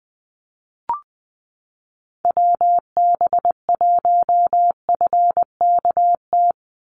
Morse code, heard as a keyed tone, about 20 wpm, 700 Hz. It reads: WB1FKT
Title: USA Call Signs 20wpm